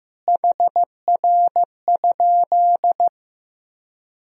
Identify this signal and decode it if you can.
Morse code, heard as a keyed tone, about 15 wpm, 700 Hz. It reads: HR?
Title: Sentences from Top 200 Words and Abbreviations 15wpm